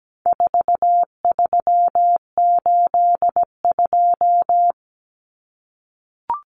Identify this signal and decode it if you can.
Morse code, heard as a keyed tone, about 17 wpm, 700 Hz. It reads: 4382
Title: Common Cut Numbers - 4-Digit 17wpm